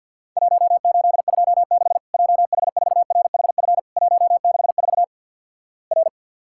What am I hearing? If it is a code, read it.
Morse code, heard as a keyed tone, about 49 wpm, 700 Hz. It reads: 1826 JH4RHF 164 R
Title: Worked All Europe DX Contest - QTC Exchange - 50wpm